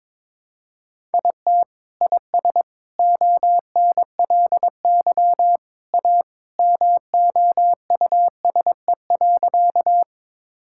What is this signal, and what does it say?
Morse code, heard as a keyed tone, about 22 wpm, 700 Hz. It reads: IT IS ONLY A MOUSE.
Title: Sentences from Top 1000 Words - Review Part 5 - 22wpm